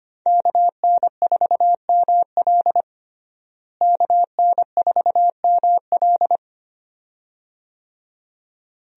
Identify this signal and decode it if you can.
Morse code, heard as a keyed tone, about 25 wpm, 700 Hz. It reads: KN4ML KN4ML
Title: USA Call Signs - 5 Characters - Repeated 2x - 25wpm